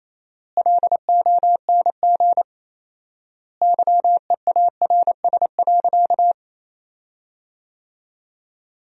Morse code, heard as a keyed tone, about 28 wpm, 700 Hz. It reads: LONG YEARS.
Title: 4x Word-Spacing - Sentences from Top 200 Words - 28wpm